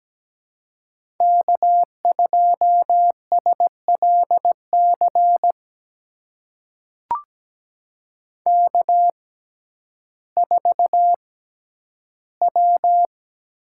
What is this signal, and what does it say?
Morse code, heard as a keyed tone, about 17 wpm, spaced as for 10 wpm, 700 Hz. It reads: K2SLC K4W
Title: USA Call Signs - 5 Characters - 6x Character Spacing - 17wpm